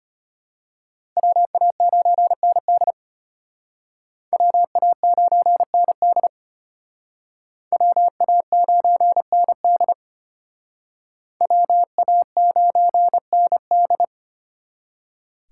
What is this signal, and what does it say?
Morse code, heard as a keyed tone, about 33 wpm, 700 Hz. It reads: WA9ND WA9ND WA9ND WA9ND